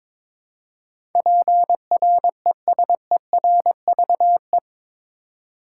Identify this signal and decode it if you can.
Morse code, heard as a keyed tone, about 22 wpm, 700 Hz. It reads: PRESERVE
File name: practice / 22wpm / Eight Letter Words - 22wpm